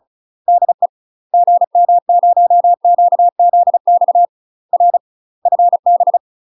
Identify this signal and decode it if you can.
Morse code, heard as a keyed tone, about 36 wpm, 700 Hz. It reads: DE GM0QZX R FB
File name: UK MASTER QSOs TRAINING / 35WPM / UK MASTER QSOs 0011